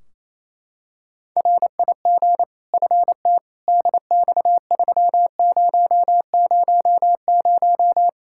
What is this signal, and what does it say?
Morse code, heard as a keyed tone, about 28 wpm, 700 Hz. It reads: RIG FT DX3000
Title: K1USN SST QSO Practice 28wpm